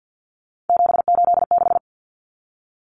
Morse code, heard as a keyed tone, about 50 wpm, 700 Hz. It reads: 786
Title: Common Cut Numbers - 3-Digit 50wpm